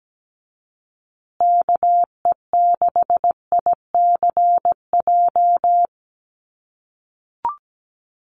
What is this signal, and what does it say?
Morse code, heard as a keyed tone, about 17 wpm, 700 Hz. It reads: KE6ICJ
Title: USA Call Signs Repeated 2x 17wpm